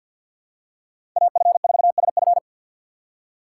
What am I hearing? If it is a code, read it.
Morse code, heard as a keyed tone, about 53 wpm, 700 Hz. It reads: AF4SF
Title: USA Call Signs - 5 Characters - 4x Character Spacing - 50wpm